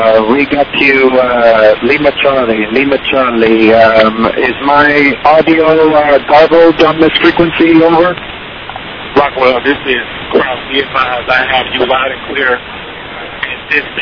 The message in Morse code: none